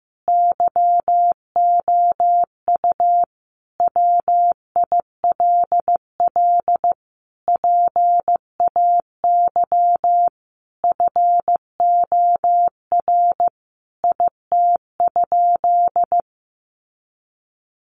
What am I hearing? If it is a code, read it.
Morse code, heard as a keyed tone, about 15 wpm, 700 Hz. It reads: YOU WILL PAY FOR IT?
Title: Sentences from Top 1300 Words 15wpm